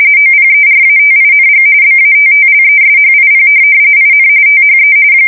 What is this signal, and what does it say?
RTTY, about 45.45 baud. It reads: OPERATING BATTLESHIP IN THE
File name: fsk170 / battleshipnj